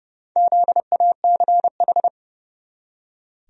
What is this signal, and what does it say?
Morse code, heard as a keyed tone, about 30 wpm, 700 Hz. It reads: ZACH
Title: USA Common Names - Max 4 Letters - 30wpm